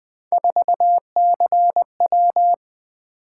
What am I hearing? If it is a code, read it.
Morse code, heard as a keyed tone, about 20 wpm, 700 Hz. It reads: 4CW
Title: Three Letter-Number Combinations - 20wpm